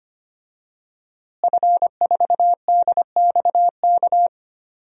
Morse code, heard as a keyed tone, about 25 wpm, 700 Hz. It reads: F4DXK
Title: DXCC-100 Call Signs - 5 Characters - Repeated 2x - 25wpm